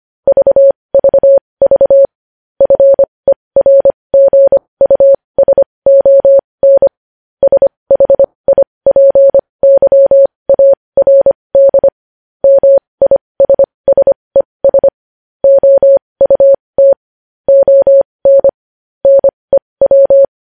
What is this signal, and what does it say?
Morse code, heard as a keyed tone, about 25 wpm, spaced as for 20 wpm, 550 Hz. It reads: VVV FERGUSON SHIPYARD MISSES OUT ON NEW
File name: podcast / News headlines-1x-25x20